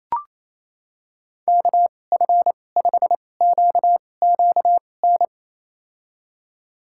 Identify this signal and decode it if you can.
Morse code, heard as a keyed tone, about 28 wpm, spaced as for 20 wpm, 700 Hz. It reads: KF5QQN
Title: USA Call Signs - 6 Characters - 2x Character Spacing - 28wpm